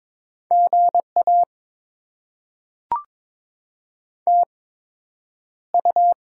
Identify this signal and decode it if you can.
Morse code, heard as a keyed tone, about 22 wpm, spaced as for 9 wpm, 700 Hz. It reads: GA TU
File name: Top 20 QSO Elements - 8x Character Spacing - 22wpm